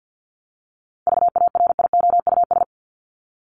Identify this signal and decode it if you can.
Morse code, heard as a keyed tone, about 50 wpm, 700 Hz. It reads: VARIOUS
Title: Top 501-750 Words 50wpm